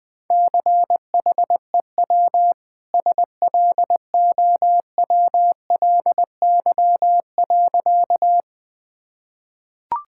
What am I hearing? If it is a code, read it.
Morse code, heard as a keyed tone, about 21 wpm, 700 Hz. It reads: CHEW SLOWLY.
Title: Easy Sentences with Hints - 20wpm